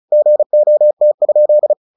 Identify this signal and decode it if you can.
Morse code, heard as a keyed tone, about 34 wpm, 600 Hz. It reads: GOT?
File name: the old man and the sea / the old man and the sea 1 35wpm